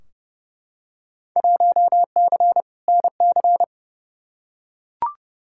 Morse code, heard as a keyed tone, about 30 wpm, 700 Hz. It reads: 1C NC
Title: Field Day Exchanges - 30wpm